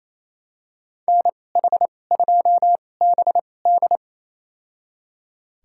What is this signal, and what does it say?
Morse code, heard as a keyed tone, about 28 wpm, spaced as for 20 wpm, 700 Hz. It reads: NH2BD
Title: USA Call Signs - 5 Characters - 2x Character Spacing - 28wpm